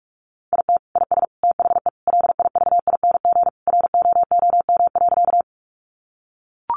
Morse code, heard as a keyed tone, about 44 wpm, 700 Hz. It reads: IT IS THE LIVING ROOM.
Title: Sentences from Top 1100 Words 45wpm